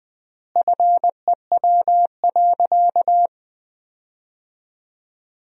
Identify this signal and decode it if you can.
Morse code, heard as a keyed tone, about 20 wpm, 700 Hz. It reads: FEW.
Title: 3x Word-Spacing - Sentences from Top 300 Words - 20wpm